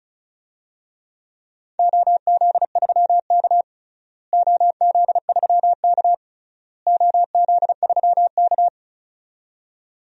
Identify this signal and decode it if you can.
Morse code, heard as a keyed tone, about 36 wpm, 700 Hz. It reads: OZ3K OZ3K OZ3K
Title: DXCC-100 Call Signs - 4 Characters - Repeated 3x - 35wpm